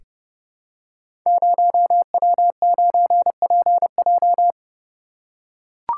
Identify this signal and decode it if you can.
Morse code, heard as a keyed tone, about 30 wpm, 700 Hz. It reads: 0W9PJ
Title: Code Groups - 5 Letter:Numbers - 30wpm